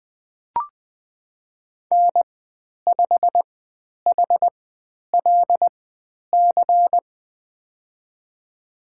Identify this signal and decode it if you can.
Morse code, heard as a keyed tone, about 20 wpm, spaced as for 10 wpm, 700 Hz. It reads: N5HLC
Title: 102 Lesson 17g - Letter u - Call Signs 20wpm